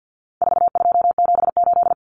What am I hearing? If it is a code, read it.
Morse code, heard as a keyed tone, about 49 wpm, 700 Hz. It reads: 4278
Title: Numbers - 4-Digit 50wpm